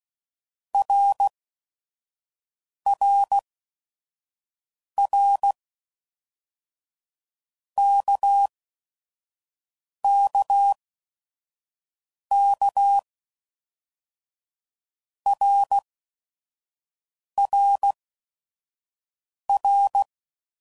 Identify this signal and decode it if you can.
Morse code, heard as a keyed tone, about 16 wpm, spaced as for 5 wpm, 800 Hz. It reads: RRR KKK RRR
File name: Basic Letter Groups By7Groups / RKPX / RKPX@5 f15